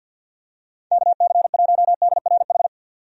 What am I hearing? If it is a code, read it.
Morse code, heard as a keyed tone, about 52 wpm, 700 Hz. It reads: KC1DRH
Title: USA Call Signs - 6 Characters - 50wpm